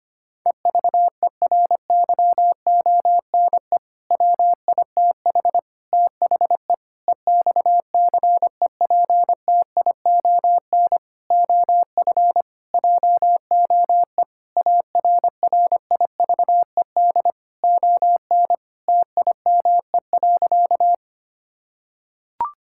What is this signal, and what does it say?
Morse code, heard as a keyed tone, about 25 wpm, 700 Hz. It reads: EVERYONE WITH THE EXCEPTION OF JOE ARRIVED ON TIME.
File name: practice / 25wpm / Sentences with 4-Word Phrases - 25wpm